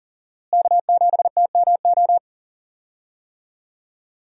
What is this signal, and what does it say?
Morse code, heard as a keyed tone, about 40 wpm, 700 Hz. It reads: K7TMO
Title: USA Call Signs - 5 Characters - Repeated 2x - 40wpm